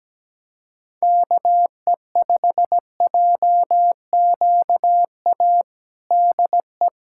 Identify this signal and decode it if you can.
Morse code, heard as a keyed tone, about 17 wpm, 700 Hz. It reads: KE5JQA DE
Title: USA Callsign DE Callsign 17wpm